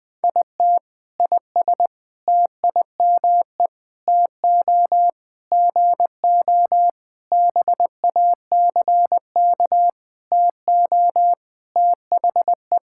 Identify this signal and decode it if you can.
Morse code, heard as a keyed tone, about 20 wpm, 700 Hz. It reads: IT IS TIME TO GO BACK TO THE
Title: Sentences with 7-Word Phrases - Spoken Beginning - 20wpm